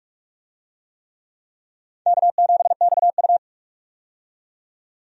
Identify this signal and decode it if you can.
Morse code, heard as a keyed tone, about 47 wpm, 700 Hz. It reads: K7XU
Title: USA Call Signs - 4 Characters - 45wpm